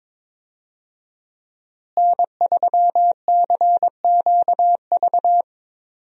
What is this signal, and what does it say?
Morse code, heard as a keyed tone, about 22 wpm, 700 Hz. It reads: N3CQV
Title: USA Call Signs - 5 Characters - Repeated 2x - 22wpm